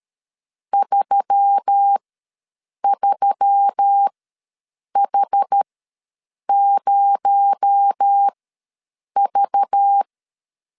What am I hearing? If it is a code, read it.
Morse code, heard as a keyed tone, about 13 wpm, spaced as for 7 wpm, 800 Hz. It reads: 33H0V
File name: LEZIONI / 8 YJ45 / 8 Y ws 12wpm 3409